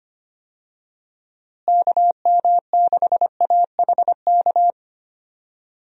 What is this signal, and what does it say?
Morse code, heard as a keyed tone, about 25 wpm, 700 Hz. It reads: KM6AHK
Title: USA Call Signs - 6 Characters - Repeated 3x - 25wpm